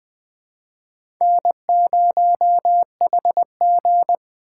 Morse code, heard as a keyed tone, about 20 wpm, 700 Hz. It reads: N0HG
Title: USA Call Signs - 4 Characters - 3x Character Spacing - 20wpm